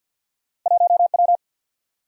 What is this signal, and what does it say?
Morse code, heard as a keyed tone, about 47 wpm, 700 Hz. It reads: 1 W
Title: Single-Letter Single-Number 50wpm